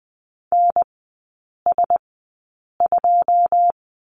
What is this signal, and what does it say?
Morse code, heard as a keyed tone, about 20 wpm, spaced as for 8 wpm, 700 Hz. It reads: NS2
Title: 205 Lesson 33e - Letter j - Three Characters 20wpm